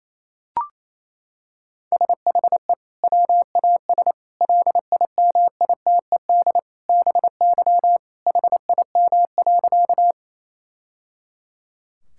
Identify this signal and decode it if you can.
Morse code, heard as a keyed tone, about 28 wpm, 700 Hz. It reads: SHE WAS LIMITED BY HIM.
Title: Sentences from Top 1000 Words - Review Part 6 - 28wpm